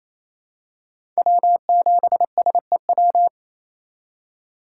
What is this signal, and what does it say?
Morse code, heard as a keyed tone, about 28 wpm, 700 Hz. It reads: W7SEW
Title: USA Call Signs - 5 Characters - 4x Character Spacing - 28wpm